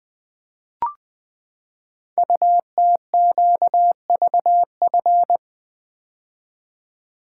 Morse code, heard as a keyed tone, about 20 wpm, 700 Hz. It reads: UTQVF
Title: Code Groups - 5 Letters - 20wpm